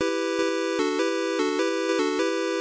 There is no code.